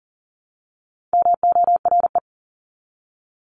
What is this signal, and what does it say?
Morse code, heard as a keyed tone, about 40 wpm, 700 Hz. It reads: MORE
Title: Top 300 Words 40wpm